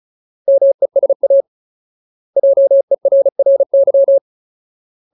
Morse code, heard as a keyed tone, about 35 wpm, 550 Hz. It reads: MESA JERRY
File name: Ultimate Word List / 35WPM Ultimate